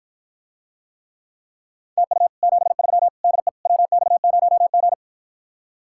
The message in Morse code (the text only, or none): TU 73 DE WX0G